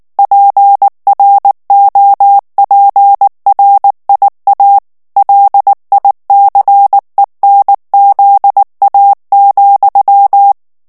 Morse code, heard as a keyed tone, about 19 wpm, 800 Hz. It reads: PROPRIA LICENZA,
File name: morse mp3 / ITALIEN texte clair 2 vitesse 18 mots minute